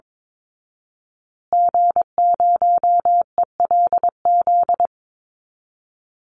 Morse code, heard as a keyed tone, about 22 wpm, 700 Hz. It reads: G0ELZ
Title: Call Signs Repeated 3x - 22wpm